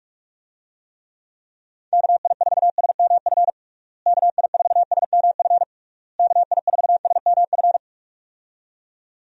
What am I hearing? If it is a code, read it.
Morse code, heard as a keyed tone, about 46 wpm, 700 Hz. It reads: KI4SMF KI4SMF KI4SMF
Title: USA Call Signs - 6 Characters - Repeated 3x - 45wpm